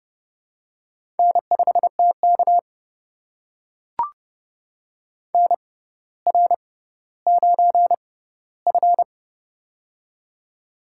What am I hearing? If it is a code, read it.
Morse code, heard as a keyed tone, about 31 wpm, spaced as for 15 wpm, 700 Hz. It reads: N5TK NR9F
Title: USA Call Signs - 4 Characters - 6x Character Spacing - 30wpm